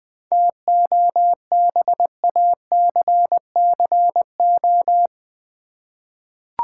Morse code, heard as a keyed tone, about 20 wpm, 700 Hz. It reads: TOBACCO
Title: 7 Letter Words - First 3 Letters Given - 20wpm